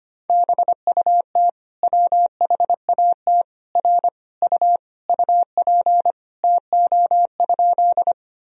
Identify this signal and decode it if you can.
Morse code, heard as a keyed tone, about 25 wpm, 700 Hz. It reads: BUT WHAT R U UP TO?
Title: Sentences from Top 100 Words and Abbreviations 25wpm